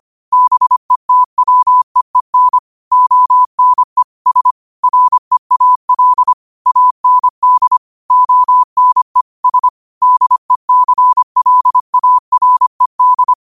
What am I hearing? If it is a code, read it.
Morse code, heard as a keyed tone, about 25 wpm, 1000 Hz. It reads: BETWEEN ONE S REAL AND ONE S DECLARED